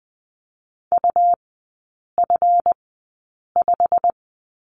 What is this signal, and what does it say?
Morse code, heard as a keyed tone, about 19 wpm, spaced as for 8 wpm, 700 Hz. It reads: UF5
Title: 138 Lesson 23e - Letter f - Three Characters 20wpm